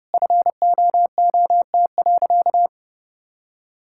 Morse code, heard as a keyed tone, about 30 wpm, 700 Hz. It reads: FOOT.
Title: Sentences from Top 1600 Words 30wpm